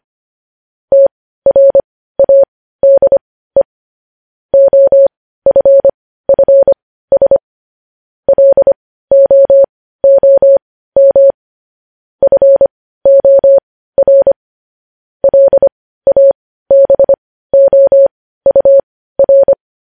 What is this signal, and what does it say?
Morse code, heard as a keyed tone, about 25 wpm, spaced as for 15 wpm, 550 Hz. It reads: TRADE OFFS LOOM FOR LABOUR
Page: news-headlines-in-morse-code-at-15-wpm